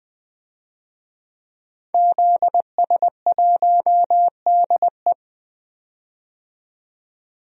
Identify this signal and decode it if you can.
Morse code, heard as a keyed tone, about 20 wpm, 700 Hz. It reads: ZS1DE
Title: DXCC-100 Call Signs - 5 Characters - 20wpm